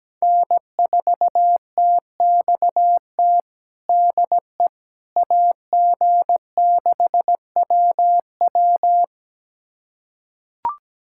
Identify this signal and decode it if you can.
Morse code, heard as a keyed tone, about 17 wpm, 700 Hz. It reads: N4TXT DE AG6WW
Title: USA Callsign DE Callsign - 5 Characters - 17wpm